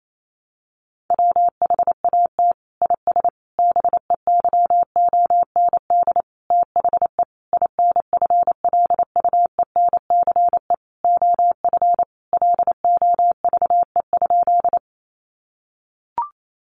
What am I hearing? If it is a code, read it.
Morse code, heard as a keyed tone, about 28 wpm, 700 Hz. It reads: WHAT IS BEYOND THE INFLUENCE OF LOVE?
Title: Sentences from Top 1100 Words 28wpm